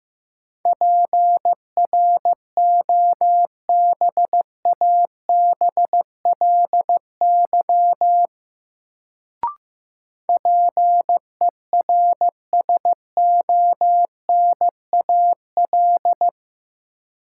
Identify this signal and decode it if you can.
Morse code, heard as a keyed tone, about 15 wpm, 700 Hz. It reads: PROBABLY PERSONAL
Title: Sentences from Top 700 Words - New Word Drill - 15wpm